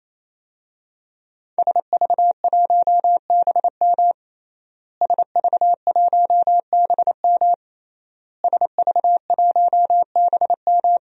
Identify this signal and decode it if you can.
Morse code, heard as a keyed tone, about 28 wpm, 700 Hz. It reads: SV1BM SV1BM SV1BM
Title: DXCC-100 Call Signs - 5 Characters - Repeated 3x - 28wpm